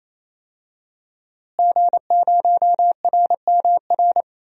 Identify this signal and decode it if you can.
Morse code, heard as a keyed tone, about 28 wpm, 700 Hz. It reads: G0RMR
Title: DXCC-100 Call Signs - 5 Characters - Repeated 3x - 28wpm